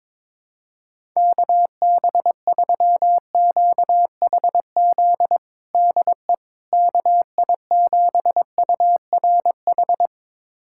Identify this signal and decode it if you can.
Morse code, heard as a keyed tone, about 22 wpm, 700 Hz. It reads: KB3QHZ DE KI7URH
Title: USA Callsign DE Callsign - 6 Characters - 22wpm